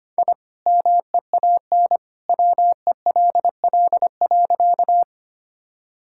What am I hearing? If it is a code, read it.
Morse code, heard as a keyed tone, about 25 wpm, 700 Hz. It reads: I MEAN WELL.
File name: practice / 25wpm / Sentences with 2-Word Phrases - 25wpm